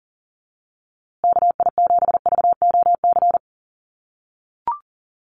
Morse code, heard as a keyed tone, about 40 wpm, 700 Hz. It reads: KI7VOC
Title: USA Call Signs Repeated 2x 40wpm